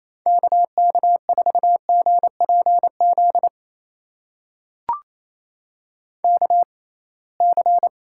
Morse code, heard as a keyed tone, about 28 wpm, spaced as for 21 wpm, 700 Hz. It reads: KK4GPZ KC
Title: USA Call Signs - 6 Characters - 6x Character Spacing - 28wpm